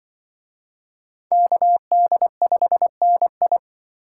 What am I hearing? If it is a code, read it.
Morse code, heard as a keyed tone, about 24 wpm, 700 Hz. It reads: KD5NI
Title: Speed Racing - USA 2x2 Call Signs 20wpm